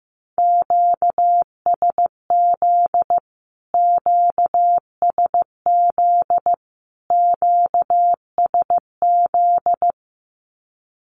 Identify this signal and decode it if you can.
Morse code, heard as a keyed tone, about 15 wpm, 700 Hz. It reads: QSZ QSZ QSZ
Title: Q-Codes Repeated 3x 15wpm